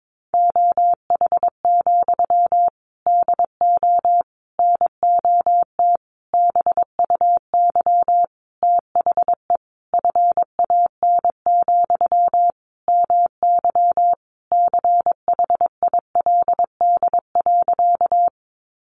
Morse code, heard as a keyed tone, about 22 wpm, 700 Hz. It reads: OH, DO NOT BUY THE FAN, MY CHILD.
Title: Sentences from Top 1300 Words 22wpm